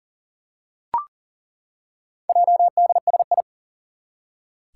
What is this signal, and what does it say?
Morse code, heard as a keyed tone, about 39 wpm, 700 Hz. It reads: JDSI
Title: Four Challenging Letter-Number Combinations 40wpm